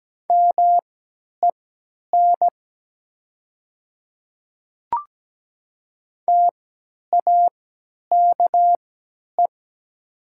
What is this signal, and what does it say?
Morse code, heard as a keyed tone, about 17 wpm, spaced as for 10 wpm, 700 Hz. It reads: MEN TAKE
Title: 3x Character Spacing - Top 200 Words - 17wpm